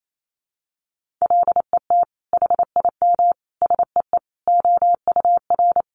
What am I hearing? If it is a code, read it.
Morse code, heard as a keyed tone, about 28 wpm, 700 Hz. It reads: LET HIM SEE OUR
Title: Sentences from Top 1400 Words 28wpm